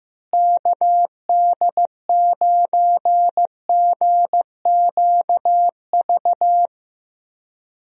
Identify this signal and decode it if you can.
Morse code, heard as a keyed tone, about 15 wpm, 700 Hz. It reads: KD9GQV
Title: USA Call Signs - 6 Characters - Repeated 3x - 15wpm